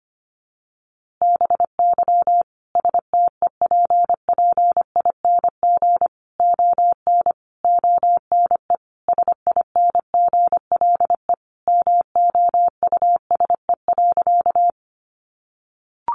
Morse code, heard as a keyed tone, about 25 wpm, 700 Hz. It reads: BY STEPPING ON ONE SINGLE MOUSE.
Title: A Sound Of Thunder 25wpm